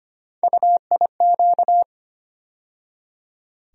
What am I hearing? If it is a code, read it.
Morse code, heard as a keyed tone, about 25 wpm, 700 Hz. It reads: UIQ
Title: USA Call Sign Postfixes - 3 Letters 25wpm